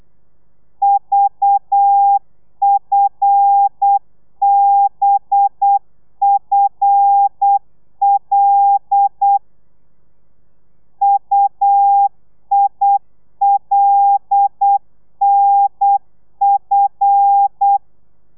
Morse code, heard as a keyed tone, about 8 wpm, 800 Hz. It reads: VFBFL UILNF
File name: telegrafia / i2tfj / Lez5 35